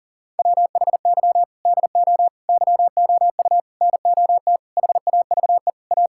Morse code, heard as a keyed tone, about 40 wpm, 700 Hz. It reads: WHY DO YOU NOT HAVE A
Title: Sentences from Top 1600 Words 40wpm